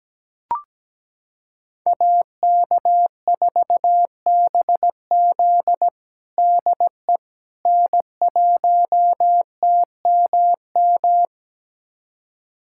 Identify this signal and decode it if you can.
Morse code, heard as a keyed tone, about 17 wpm, 700 Hz. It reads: AK4BZ DE N1TMM
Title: USA Callsign DE Callsign - 5 Characters - 17wpm